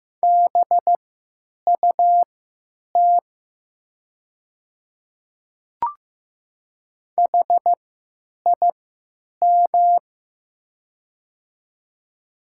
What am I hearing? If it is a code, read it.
Morse code, heard as a keyed tone, about 15 wpm, spaced as for 9 wpm, 700 Hz. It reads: BUT HIM
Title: 3x Character Spacing - Top 100 Words - 15wpm